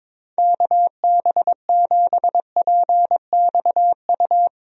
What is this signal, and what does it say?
Morse code, heard as a keyed tone, about 22 wpm, 700 Hz. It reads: KB7PXU
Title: USA Call Signs - 6 Characters - Repeated 3x - 22wpm